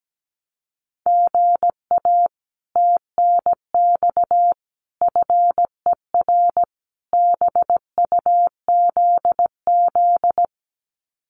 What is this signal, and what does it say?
Morse code, heard as a keyed tone, about 17 wpm, 700 Hz. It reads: GA TNX FER BUZZ
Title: Common QSO Phrases Encore 17wpm